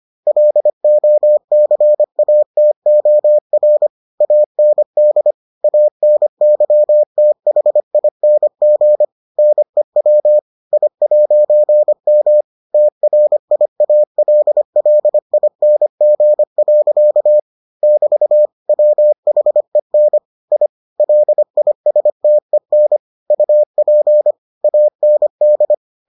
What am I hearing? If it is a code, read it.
Morse code, heard as a keyed tone, about 25 wpm, 600 Hz. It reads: LOCATOR AND ANYTHING NEW I'M TRIALLING. <BT> WHEN I LISTEN UP AND